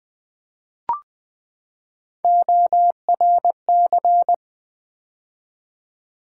Morse code, heard as a keyed tone, about 20 wpm, 700 Hz. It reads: ORC
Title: Build A Word - 3 Letters At A Time - 9-Letter Words - 20wpm